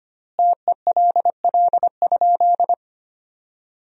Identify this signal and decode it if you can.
Morse code, heard as a keyed tone, about 25 wpm, 700 Hz. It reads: TELL?
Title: 3x Word-Spacing - Sentences from Top 300 Words - 25wpm